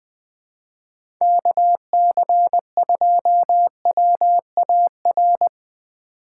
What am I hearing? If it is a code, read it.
Morse code, heard as a keyed tone, about 21 wpm, 700 Hz. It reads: KC2WAR
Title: USA Call Signs - 6 Characters - 3x Character Spacing - 20wpm